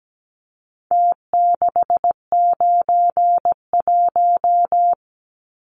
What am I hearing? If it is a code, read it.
Morse code, heard as a keyed tone, about 17 wpm, 700 Hz. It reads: T691
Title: Numbers - Contest Style - 4-Digit 17wpm